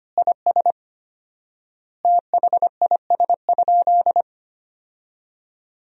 Morse code, heard as a keyed tone, about 25 wpm, 700 Hz. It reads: IS THIS?
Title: 4x Word-Spacing - Sentences from Top 100 Words - 25wpm